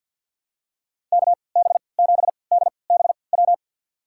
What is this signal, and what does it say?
Morse code, heard as a keyed tone, about 52 wpm, spaced as for 29 wpm, 700 Hz. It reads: KB7DBW
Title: USA Call Signs - 6 Characters - 3x Character Spacing - 50wpm